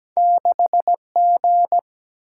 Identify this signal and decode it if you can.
Morse code, heard as a keyed tone, about 17 wpm, 700 Hz. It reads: 6G
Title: Single-Letter Single-Number 17wpm